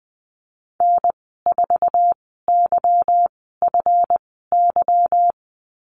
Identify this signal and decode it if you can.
Morse code, heard as a keyed tone, about 20 wpm, spaced as for 14 wpm, 700 Hz. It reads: N4YFY
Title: 154 Lesson 25g - Letter p - Call Signs 20wpm